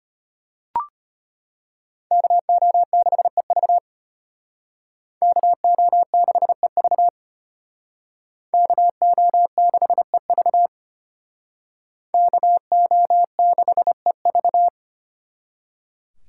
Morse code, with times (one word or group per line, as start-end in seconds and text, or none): 2.1-3.8 KO6EV
5.2-7.1 KO6EV
8.5-10.7 KO6EV
12.1-14.7 KO6EV